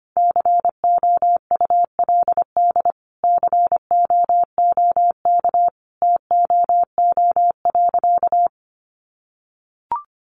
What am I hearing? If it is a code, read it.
Morse code, heard as a keyed tone, about 25 wpm, 700 Hz. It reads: COULD COOK TOO.
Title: Sentences from Top 700 Words 25wpm